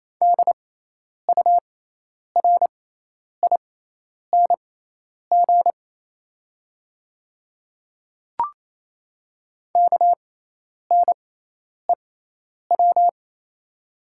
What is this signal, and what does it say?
Morse code, heard as a keyed tone, about 27 wpm, spaced as for 10 wpm, 700 Hz. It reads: DURING KNEW
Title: 6x Character Spacing - Top 500 Words - 28wpm